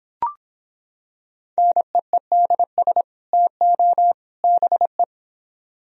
Morse code, heard as a keyed tone, about 26 wpm, 700 Hz. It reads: NEEDS TO BE